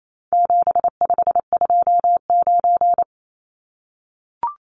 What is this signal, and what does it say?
Morse code, heard as a keyed tone, about 28 wpm, 700 Hz. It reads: 7529
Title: Numbers - 4-Digit 28wpm